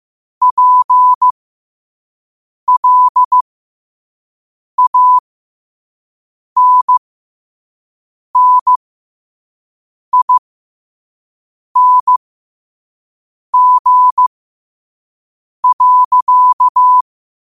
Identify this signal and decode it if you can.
Morse code, heard as a keyed tone, about 15 wpm, spaced as for 5 wpm, 1000 Hz. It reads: PLANNING.
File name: 2024-10-07-5wpm